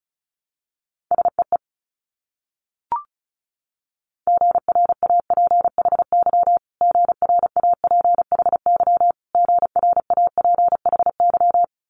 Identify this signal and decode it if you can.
Morse code, heard as a keyed tone, about 36 wpm, 700 Hz. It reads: SEE GRAPHY GRAPHY GRAPHY
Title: Multi-letter Phonemes Repeated 3x 35wpm